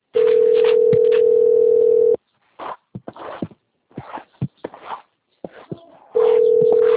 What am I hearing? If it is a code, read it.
Ringback tone (North America), 440 + 480 Hz.
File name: out- - - - - .11218